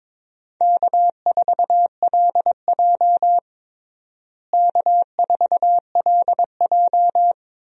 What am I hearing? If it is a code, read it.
Morse code, heard as a keyed tone, about 23 wpm, 700 Hz. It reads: K4LJ K4LJ
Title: USA Call Signs - 4 Characters - Repeated 3x - 22wpm